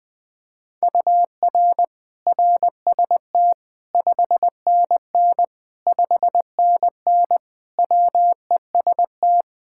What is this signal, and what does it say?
Morse code, heard as a keyed tone, about 20 wpm, 700 Hz. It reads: UR RST 5NN 5NN WEST
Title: K1USN SST QSO Practice 20wpm